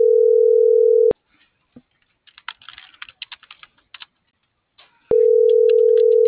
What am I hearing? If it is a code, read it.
Ringback tone (North America), 440 + 480 Hz.